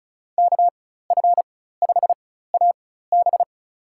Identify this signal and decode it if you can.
Morse code, heard as a keyed tone, about 36 wpm, spaced as for 16 wpm, 700 Hz. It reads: KF5AB
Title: USA Call Signs - 5 Characters - 4x Character Spacing - 35wpm